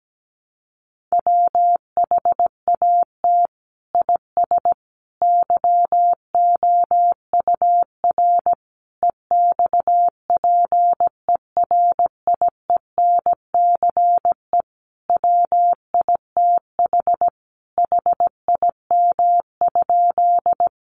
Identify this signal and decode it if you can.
Morse code, heard as a keyed tone, about 17 wpm, 700 Hz. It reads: WHAT IS YOUR EXPERIENCE WITH HIM?
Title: Sentences from Top 600 Words 17wpm